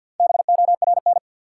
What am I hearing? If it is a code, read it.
Morse code, heard as a keyed tone, about 53 wpm, 700 Hz. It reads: BORN
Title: Four Letter Words - 50wpm